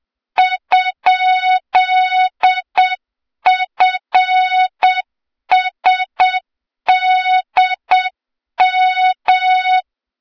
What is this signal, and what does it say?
Morse code, heard as a keyed tone, about 8 wpm, 3700 Hz, with noise. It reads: ?FSDM
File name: Morze-HA2MN / hang / 23 GYAKORLAT